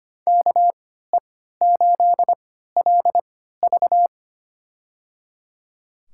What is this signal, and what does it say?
Morse code, heard as a keyed tone, about 25 wpm, spaced as for 14 wpm, 700 Hz. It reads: KE8LV
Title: USA Call Signs - 5 Characters - 3x Character Spacing - 25wpm